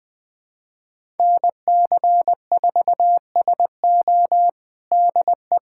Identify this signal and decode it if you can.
Morse code, heard as a keyed tone, about 20 wpm, 700 Hz. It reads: NC4SO DE
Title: USA Callsign DE Callsign - 5 Characters - 20wpm